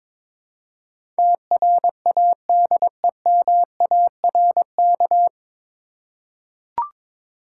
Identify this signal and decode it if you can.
Morse code, heard as a keyed tone, about 22 wpm, 690 Hz. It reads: TRADEMARK
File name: 9 Letter Words - First 3 Letters Given - 22wpm